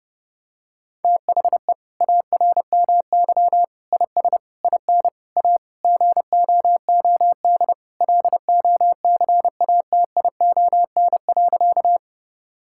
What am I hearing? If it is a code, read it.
Morse code, heard as a keyed tone, about 32 wpm, 700 Hz. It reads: THE ARMY IS IN A GOOD LOCATION.
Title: Sentences from Top 1000 Words - Review Part 6 - 30wpm